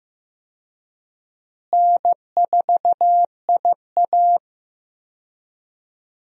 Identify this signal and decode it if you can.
Morse code, heard as a keyed tone, about 15 wpm, 700 Hz. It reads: N4IA
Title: USA Call Signs - 4 Characters - Repeated 2x - 15wpm